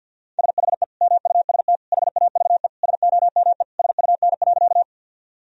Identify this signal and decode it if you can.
Morse code, heard as a keyed tone, about 50 wpm, 700 Hz. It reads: SHE MUST HAVE SOME SUN.
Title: Sentences from Top 1000 Words - Review Part 1 - 50wpm